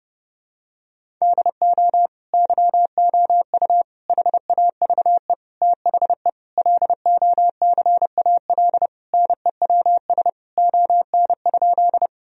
Morse code, heard as a keyed tone, about 30 wpm, 700 Hz. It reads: DO YOU HAVE THE LOCAL NEWS ON?
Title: Sentences from Top 1000 Words - Review Part 3 - 30wpm